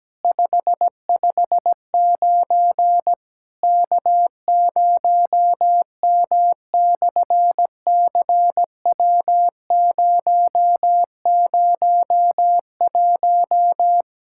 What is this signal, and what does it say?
Morse code, heard as a keyed tone, about 17 wpm, 700 Hz. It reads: 559 K0M/CW001
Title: SOTA QSO Practice 17wpm